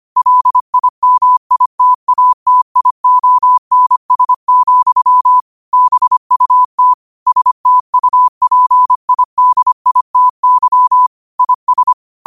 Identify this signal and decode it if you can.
Morse code, heard as a keyed tone, about 25 wpm, 1000 Hz. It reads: LIMITATIONS, BUT STUPIDITY IS
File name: qotdfiles / 25wpm